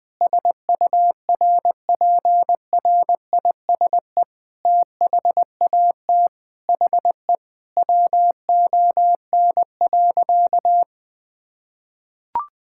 Morse code, heard as a keyed tone, about 19 wpm, 700 Hz. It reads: SURPRISE THAT HE WON.
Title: Sentences with 6-Word Phrases - 20wpm